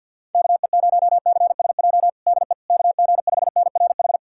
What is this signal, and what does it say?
Morse code, heard as a keyed tone, about 51 wpm, 700 Hz. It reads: KE0CSJ DE KG5NRH
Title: USA Callsign DE Callsign - 6 Characters - 50wpm